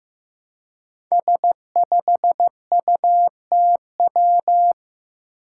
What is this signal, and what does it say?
Morse code, heard as a keyed tone, about 15 wpm, 700 Hz. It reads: S5UTW
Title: Five Challenging Letter-Number Combinations 15wpm